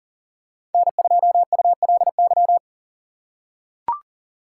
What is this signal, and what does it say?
Morse code, heard as a keyed tone, about 40 wpm, 700 Hz. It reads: N2ULY
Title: USA Call Signs - 6x Character Spacing - 40wpm